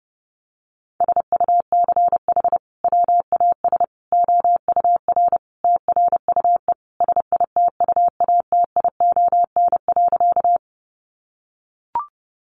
Morse code, heard as a keyed tone, about 30 wpm, 700 Hz. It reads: SUCH WAS OUR TRUE SITUATION.
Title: Sentences from Top 1200 Words 30wpm